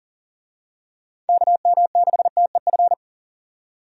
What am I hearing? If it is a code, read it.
Morse code, heard as a keyed tone, about 40 wpm, 700 Hz. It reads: KM6TEF
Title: USA Call Signs - 6 Characters - 40wpm